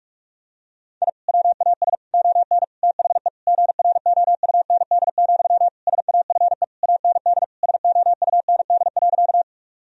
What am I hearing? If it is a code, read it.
Morse code, heard as a keyed tone, about 46 wpm, 700 Hz. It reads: I WAS ON THE GROUND, SAFE AND SOUND.
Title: Sentences from Top 1000 Words - Review Part 2 - 45wpm